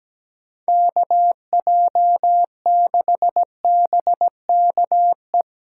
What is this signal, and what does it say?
Morse code, heard as a keyed tone, about 17 wpm, 700 Hz. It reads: KJ6BKE
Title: USA Callsign DE Callsign - 6 Characters - 17wpm